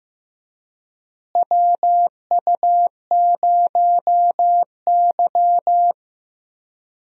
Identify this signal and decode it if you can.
Morse code, heard as a keyed tone, about 15 wpm, 700 Hz. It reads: WU0Y
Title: USA Call Signs - 4 Characters - Repeated 2x - 15wpm